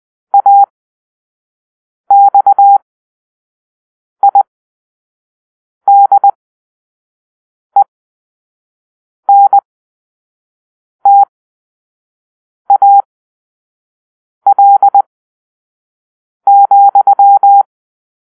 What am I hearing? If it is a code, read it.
Morse code, heard as a keyed tone, about 20 wpm, spaced as for 5 wpm, 800 Hz. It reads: AXIDENTAL,
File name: arrl / Grp-7 Drill-3